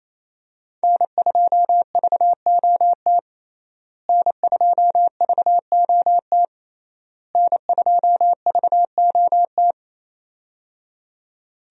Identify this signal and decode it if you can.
Morse code, heard as a keyed tone, about 28 wpm, 700 Hz. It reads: N2VOT N2VOT N2VOT
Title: USA Call Signs - 5 Characters - Repeated 3x - 28wpm